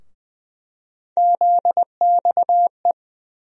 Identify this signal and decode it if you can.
Morse code, heard as a keyed tone, about 20 wpm, 700 Hz. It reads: ZXE
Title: USA Call Sign Postfixes - 3 Letters 20wpm